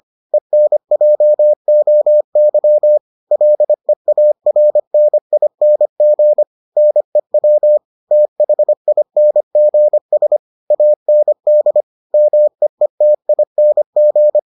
Morse code, heard as a keyed tone, about 25 wpm, 600 Hz. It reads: ENJOY LEARNING NEW THINGS AND MEETING